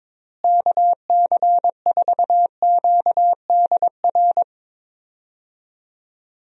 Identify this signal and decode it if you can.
Morse code, heard as a keyed tone, about 22 wpm, 700 Hz. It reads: KC4QDR
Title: USA Call Signs - 6 Characters - Repeated 2x - 22wpm